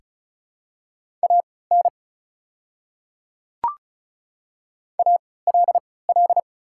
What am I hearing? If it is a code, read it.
Morse code, heard as a keyed tone, about 36 wpm, spaced as for 20 wpm, 690 Hz. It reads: AN ALL
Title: 3x Character Spacing - Top 100 Words - 35wpm